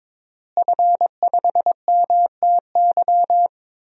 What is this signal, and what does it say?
Morse code, heard as a keyed tone, about 21 wpm, 700 Hz. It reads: F5MTY
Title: DXCC-100 Call Signs - 5 Characters - Repeated 3x - 22wpm